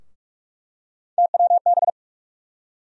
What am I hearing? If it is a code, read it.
Morse code, heard as a keyed tone, about 42 wpm, 700 Hz. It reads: TWB
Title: USA Call Sign Postfixes - 3 Letters 45wpm